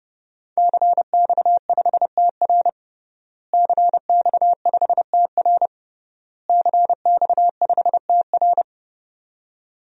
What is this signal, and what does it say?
Morse code, heard as a keyed tone, about 32 wpm, 700 Hz. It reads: CX5TR CX5TR CX5TR
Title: DXCC-100 Call Signs - 5 Characters - Repeated 3x - 30wpm